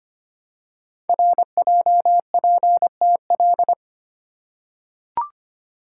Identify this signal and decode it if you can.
Morse code, heard as a keyed tone, about 25 wpm, 700 Hz. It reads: RJPTL
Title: Code Groups - 5 Letters - 25wpm